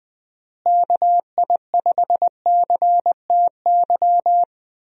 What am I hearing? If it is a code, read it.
Morse code, heard as a keyed tone, about 21 wpm, 700 Hz. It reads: KI5CTY
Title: USA Call Signs - 6 Characters - Repeated 3x - 20wpm